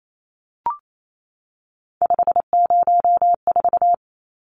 Morse code, heard as a keyed tone, about 28 wpm, 700 Hz. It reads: 504
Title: Numbers - 3-Digit 28wpm